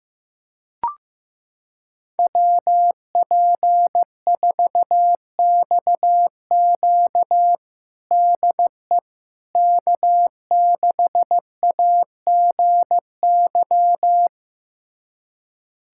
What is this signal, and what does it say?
Morse code, heard as a keyed tone, about 15 wpm, 700 Hz. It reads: WP4XQ DE K6AGY
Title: USA Callsign DE Callsign 15wpm